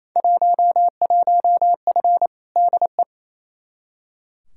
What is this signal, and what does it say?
Morse code, heard as a keyed tone, about 28 wpm, 700 Hz. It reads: 11F DE
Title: Field Day Exchanges Encore - 28wpm